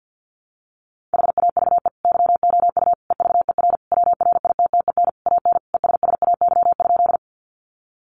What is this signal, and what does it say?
Morse code, heard as a keyed tone, about 50 wpm, 700 Hz. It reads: HAVE YOU EVER WRITTEN AN ESSAY?
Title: Sentences from Top 1500 Words 50wpm